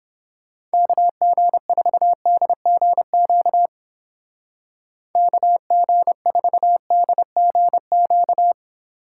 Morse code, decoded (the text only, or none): KG4DGQ KG4DGQ